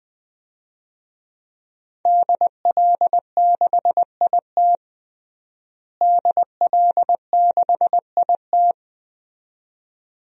Morse code, heard as a keyed tone, about 20 wpm, 700 Hz. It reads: DL6IT DL6IT
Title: DXCC-100 Call Signs - 5 Characters - Repeated 2x - 20wpm